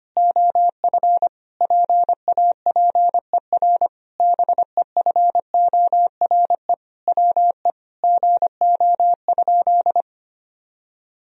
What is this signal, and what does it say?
Morse code, heard as a keyed tone, about 25 wpm, 700 Hz. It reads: OF PAPER BEFORE WE GO?
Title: Sentences from Top 1000 Words - Review Part 2 - 25wpm